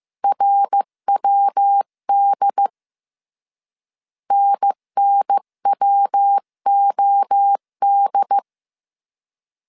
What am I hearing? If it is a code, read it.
Morse code, heard as a keyed tone, about 15 wpm, 800 Hz. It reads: RWD NNWOD
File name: CW / DET / 1 3 1500C S C 14wpm 2458